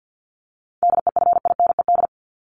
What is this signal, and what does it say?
Morse code, heard as a keyed tone, about 50 wpm, 700 Hz. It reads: DEFINED
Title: ED Suffix - Top 100 Words with Base Words 50wpm